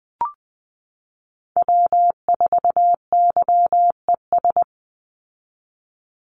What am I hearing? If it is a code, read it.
Morse code, heard as a keyed tone, about 20 wpm, 700 Hz. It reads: W4YES
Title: Call Signs Repeated 3x - 20wpm